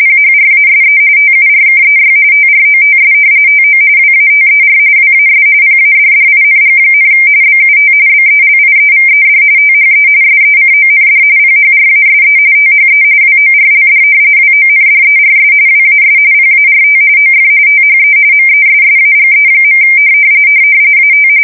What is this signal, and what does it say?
RTTY, about 45.45 baud. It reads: PHRASE TURKEY TROTS TO WATER WAS PADDING, BUT BELIEVED THE SECOND PHRASE THE WORLD WONDERS WAS PART OF THE MESSAGE. AT
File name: fsk170 / battleshipnj